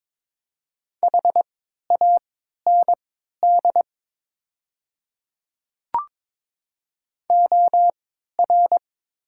Morse code, heard as a keyed tone, about 22 wpm, spaced as for 13 wpm, 700 Hz. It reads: HAND OR
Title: 3x Character Spacing - Top 200 Words - 22wpm